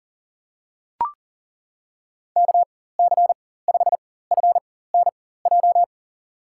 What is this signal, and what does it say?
Morse code, heard as a keyed tone, about 40 wpm, spaced as for 19 wpm, 700 Hz. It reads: KC5FNJ
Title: USA Call Signs - 6 Characters - 4x Character Spacing - 40wpm